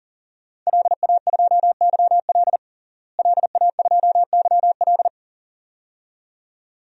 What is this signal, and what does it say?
Morse code, heard as a keyed tone, about 40 wpm, 700 Hz. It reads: LA2YL LA2YL
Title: DXCC-100 Call Signs - 5 Characters - Repeated 2x - 40wpm